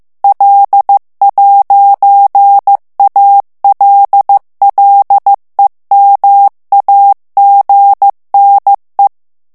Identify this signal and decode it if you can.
Morse code, heard as a keyed tone, about 15 wpm, 800 Hz. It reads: L'ALLEMAGNE
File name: morse mp3 / FRANCAIS TEXTE CLAIR 1 VITESSE 14 MOTS MINUTES